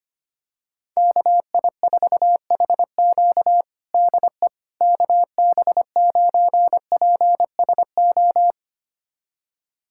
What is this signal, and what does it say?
Morse code, heard as a keyed tone, about 25 wpm, 700 Hz. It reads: KI4HQ DE KB9PSO
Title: USA Callsign DE Callsign 25wpm